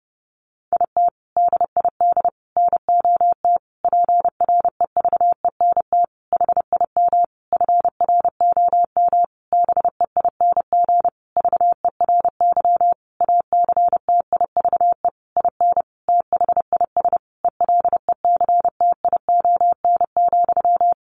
Morse code, heard as a keyed tone, about 28 wpm, 700 Hz. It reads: IT DID NOT PREVENT HIM FROM BEING VERY ACTIVE IN THIS ELECTION,
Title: Sentences from Top 1500 Words 30wpm